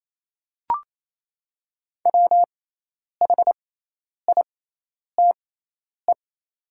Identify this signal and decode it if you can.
Morse code, heard as a keyed tone, about 28 wpm, spaced as for 10 wpm, 690 Hz. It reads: WHITE
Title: 6x Character Spacing - Top 300 Words - 28wpm